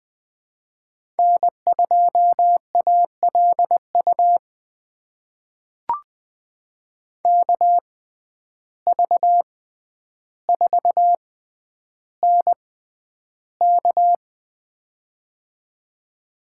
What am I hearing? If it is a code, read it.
Morse code, heard as a keyed tone, about 20 wpm, spaced as for 10 wpm, 700 Hz. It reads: N2ALU KV4NK
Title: USA Call Signs - 5 Characters - 6x Character Spacing - 20wpm